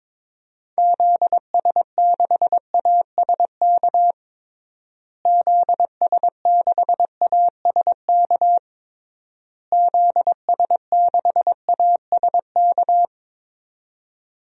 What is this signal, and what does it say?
Morse code, heard as a keyed tone, about 22 wpm, 700 Hz. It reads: ZS6ASK ZS6ASK ZS6ASK
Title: DXCC-100 Call Signs - 6 Characters - Repeated 3x - 22wpm